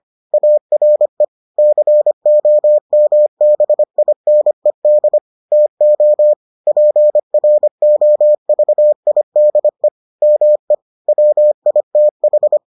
Morse code, heard as a keyed tone, about 25 wpm, 600 Hz. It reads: ARE COMBINED TO PROVIDE ME WITH